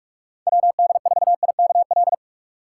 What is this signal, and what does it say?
Morse code, heard as a keyed tone, about 44 wpm, 700 Hz. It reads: WD4IKL